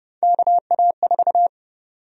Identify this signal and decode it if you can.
Morse code, heard as a keyed tone, about 30 wpm, 700 Hz. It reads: KA4
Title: USA Call Sign Prefixes - 30wpm